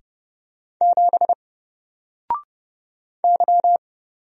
Morse code, heard as a keyed tone, about 30 wpm, 700 Hz. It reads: 7 Y
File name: Single-Letter-Number-30wpm